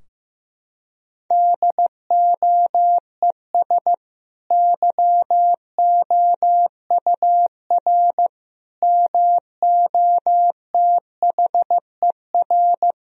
Morse code, heard as a keyed tone, about 16 wpm, 700 Hz. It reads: DOES YOUR MOTHER
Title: Sentences from Top 1000 Words - Review Part 4 - 15wpm